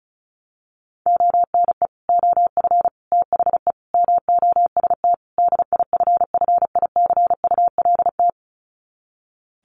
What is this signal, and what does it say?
Morse code, heard as a keyed tone, about 34 wpm, 700 Hz. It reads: ONE OF THE MOST DIFFICULT
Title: Sets of 5 Words Encore2 35wpm